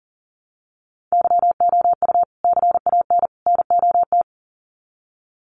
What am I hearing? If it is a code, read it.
Morse code, heard as a keyed tone, about 40 wpm, 700 Hz. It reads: YOU CAN NOT
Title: Sets of 3 Words 40wpm